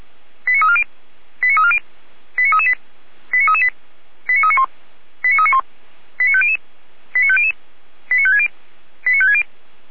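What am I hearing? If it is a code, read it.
Five-tone selective call (ZVEI), heard as quick runs of five tones, tones of about 70 ms. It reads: 89340 89340 89338 89338 89391 89391 89500 89500 89560 89560
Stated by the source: ZVEI five-tone call received on an SDR.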